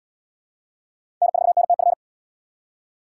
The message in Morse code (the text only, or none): N4NTF